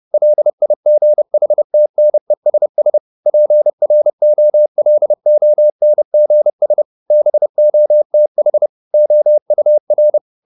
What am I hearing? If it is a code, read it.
Morse code, heard as a keyed tone, about 30 wpm, 600 Hz. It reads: LIGHTNESS PROLONGS BOTH OUR